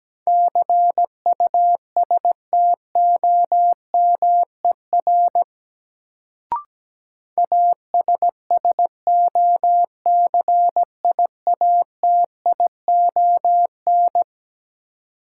Morse code, heard as a keyed tone, about 17 wpm, 700 Hz. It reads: CUSTOMER ASSOCIATION
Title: Sentences from Top 1100 Words - New Word Drill Alt - 17wpm